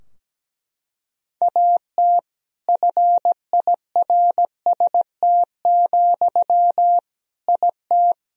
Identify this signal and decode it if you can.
Morse code, heard as a keyed tone, about 17 wpm, 700 Hz. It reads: AT FIRST, IT
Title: Sentences from Top 1000 Words - Review Part 4 - 17wpm